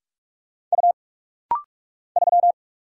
Morse code, heard as a keyed tone, about 44 wpm, 700 Hz. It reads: U 3
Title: Single Letter-Number - Rapid-Fire 45wpm